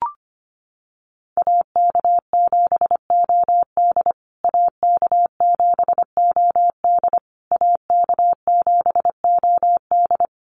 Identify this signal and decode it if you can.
Morse code, heard as a keyed tone, about 25 wpm, 700 Hz. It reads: AK7OD AK7OD AK7OD
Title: USA Call Signs Repeated 3x 25wpm